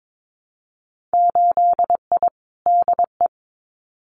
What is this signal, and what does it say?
Morse code, heard as a keyed tone, about 22 wpm, 700 Hz. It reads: 8I DE
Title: Winter Field Day Exchanges Encore 22wpm